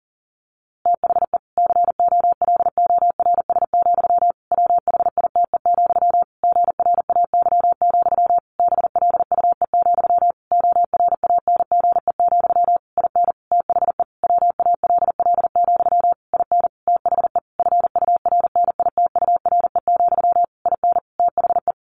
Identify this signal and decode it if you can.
Morse code, heard as a keyed tone, about 40 wpm, 700 Hz. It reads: THE COLORS, WHITE, GRAY, BLUE, ORANGE, IN THE WALL, IN THE FURNITURE, IN THE
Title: A Sound Of Thunder 40wpm